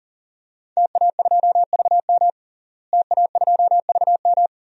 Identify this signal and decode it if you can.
Morse code, heard as a keyed tone, about 41 wpm, 700 Hz. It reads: TA2VM TA2VM
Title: DXCC-100 Call Signs - 5 Characters - Repeated 3x - 40wpm